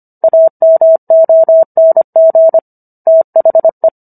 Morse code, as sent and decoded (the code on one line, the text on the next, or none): .- -- --- -. --. / - .... .
AMONG THE